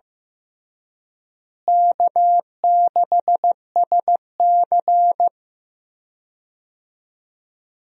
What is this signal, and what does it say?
Morse code, heard as a keyed tone, about 15 wpm, 700 Hz. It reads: K6SC
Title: USA Call Signs - 4 Characters - Repeated 2x - 15wpm